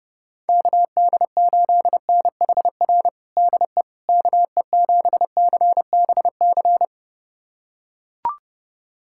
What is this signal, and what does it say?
Morse code, heard as a keyed tone, about 32 wpm, 700 Hz. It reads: KD8NHR DE KE7CBC
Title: USA Callsign DE Callsign 30wpm